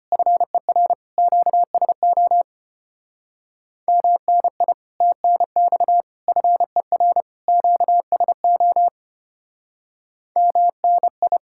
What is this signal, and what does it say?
Morse code, heard as a keyed tone, about 31 wpm, 700 Hz. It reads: FER QSO MNI TNX FER QSO MNI